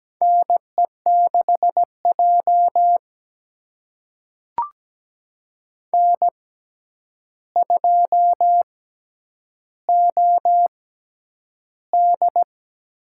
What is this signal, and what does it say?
Morse code, heard as a keyed tone, about 17 wpm, spaced as for 9 wpm, 700 Hz. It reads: NE6J N2OD
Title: USA Call Signs - 4 Characters - 6x Character Spacing - 17wpm